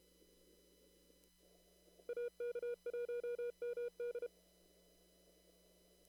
Morse code, heard as a keyed tone, about 32 wpm, 480 Hz. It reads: AK1MD
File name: CW / kt5j wpx cw-op-k5pi